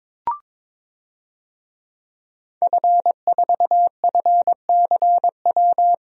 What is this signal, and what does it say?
Morse code, heard as a keyed tone, about 22 wpm, 700 Hz. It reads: F4FCW
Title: DXCC-100 Call Signs - 5 Characters - Repeated 3x - 22wpm